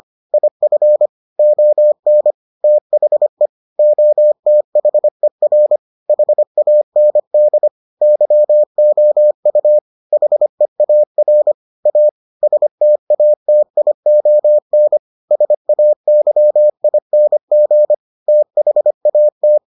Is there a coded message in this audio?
Yes: Morse code from 0.3 to 19.6 s, heard as a keyed tone, about 25 wpm, 600 Hz. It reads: IF ON THE OTHER HAND YOU HEAR A STATION SAYING THAT